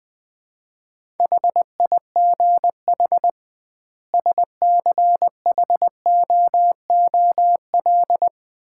Morse code, heard as a keyed tone, about 20 wpm, 700 Hz. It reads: HIGH SCHOOL
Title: 2x Word-Spacing - Sets of 2 Words - 20wpm